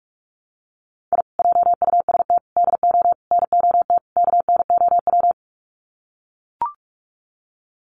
Morse code, heard as a keyed tone, about 44 wpm, 700 Hz. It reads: I JUST DO NOT KNOW
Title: Sets of 5 Words Encore 45wpm